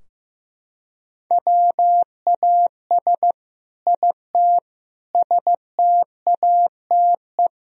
Morse code, heard as a keyed tone, about 16 wpm, 700 Hz. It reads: WAS IT STATE
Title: Sentences from Top 1000 Words - Review Part 3 - 15wpm